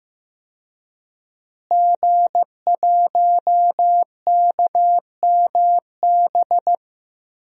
Morse code, heard as a keyed tone, about 15 wpm, 700 Hz. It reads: G1KMB
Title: DXCC-100 Call Signs - 5 Characters - Repeated 2x - 15wpm